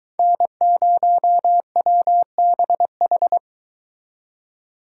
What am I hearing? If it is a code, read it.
Morse code, heard as a keyed tone, about 23 wpm, 700 Hz. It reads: N0WBH